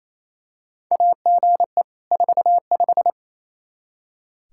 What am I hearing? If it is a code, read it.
Morse code, heard as a keyed tone, about 28 wpm, 700 Hz. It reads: AGE 45
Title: Practice QSO - Exchange 3 - More Info - 28wpm